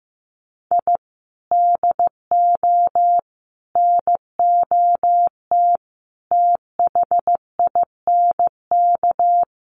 Morse code, heard as a keyed tone, about 15 wpm, 700 Hz. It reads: I DO NOT THINK
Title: Sets of 5 Words Encore 15wpm